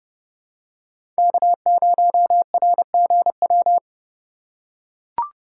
Morse code, heard as a keyed tone, about 30 wpm, 690 Hz. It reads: K0RGW
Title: USA Call Signs - 5 Characters - 6x Character Spacing - 30wpm